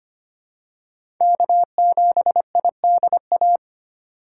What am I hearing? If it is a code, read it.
Morse code, heard as a keyed tone, about 25 wpm, 700 Hz. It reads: K7IDA
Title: USA Call Signs - 5 Characters - 4x Character Spacing - 25wpm